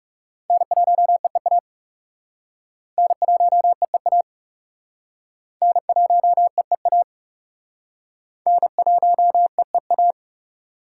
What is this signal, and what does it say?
Morse code, heard as a keyed tone, about 37 wpm, 700 Hz. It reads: N1EEA N1EEA N1EEA N1EEA
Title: Speed Racing - USA 1x3 Call Signs 30wpm